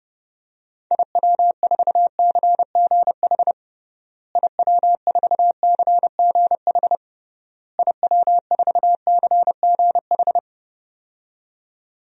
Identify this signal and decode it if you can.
Morse code, heard as a keyed tone, about 29 wpm, 700 Hz. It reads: IW4CGH IW4CGH IW4CGH
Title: DXCC-100 Call Signs - 6 Characters - Repeated 3x - 30wpm